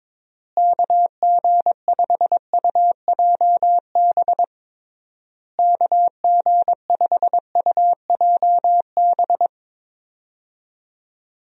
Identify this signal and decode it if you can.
Morse code, heard as a keyed tone, about 21 wpm, 700 Hz. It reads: KG5UJB KG5UJB
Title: USA Call Signs - 6 Characters - Repeated 3x - 22wpm